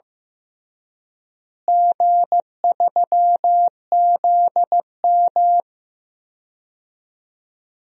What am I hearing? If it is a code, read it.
Morse code, heard as a keyed tone, about 15 wpm, 700 Hz. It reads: G3ZM
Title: DXCC-100 Call Signs - 4 Characters - Repeated 3x - 15wpm